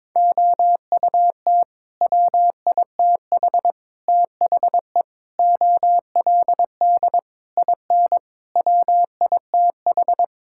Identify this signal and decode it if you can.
Morse code, heard as a keyed tone, about 22 wpm, 700 Hz. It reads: OUT WITH THE OLD IN WITH
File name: practice / 22wpm / Sets of 8 Words - 22wpm